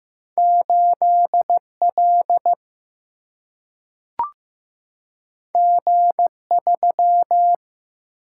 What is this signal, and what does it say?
Morse code, heard as a keyed tone, about 15 wpm, 700 Hz. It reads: 8L G3
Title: Single-Letter Single-Number 15wpm